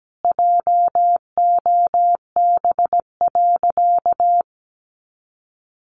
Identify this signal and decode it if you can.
Morse code, heard as a keyed tone, about 17 wpm, 700 Hz. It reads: JOB.
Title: Sentences from Top 700 Words 17wpm